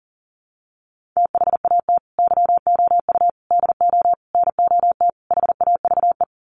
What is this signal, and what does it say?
Morse code, heard as a keyed tone, about 37 wpm, 700 Hz. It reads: THAT YOU DO NOT HAVE
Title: Sets of 5 Words Encore 40wpm